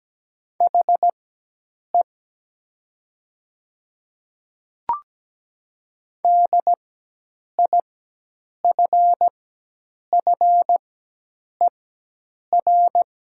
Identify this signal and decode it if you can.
Morse code, heard as a keyed tone, about 17 wpm, spaced as for 8 wpm, 700 Hz. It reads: HE DIFFER
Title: 4x Character Spacing - Top 200 Words - 17wpm